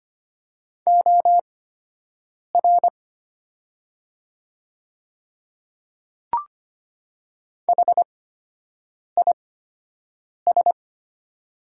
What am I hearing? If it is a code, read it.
Morse code, heard as a keyed tone, about 25 wpm, spaced as for 7 wpm, 700 Hz. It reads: OR HIS
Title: 8x Character Spacing - Top 100 Words - 25wpm